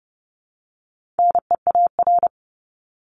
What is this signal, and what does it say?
Morse code, heard as a keyed tone, about 30 wpm, 700 Hz. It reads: NEAR
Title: Top 200 Words Repeated 30wpm